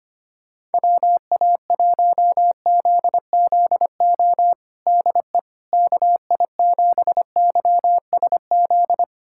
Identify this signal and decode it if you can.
Morse code, heard as a keyed tone, about 25 wpm, 700 Hz. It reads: WA1ZZO DE KI7YSZ
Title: USA Callsign DE Callsign - 6 Characters - 25wpm